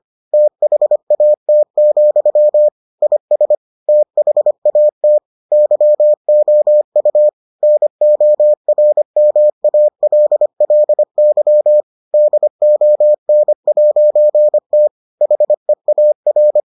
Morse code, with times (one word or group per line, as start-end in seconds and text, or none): 0.3-2.7 THAT,
3.0-3.5 IS
3.9-5.2 THAT
5.5-7.3 YOU
7.6-11.8 NORMALLY
12.1-14.9 DON'T
15.2-16.6 HEAR